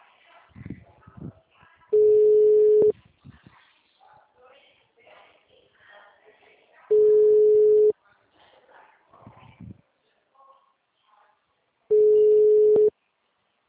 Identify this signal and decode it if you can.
Ringback tone (Europe, 425 Hz), with speech.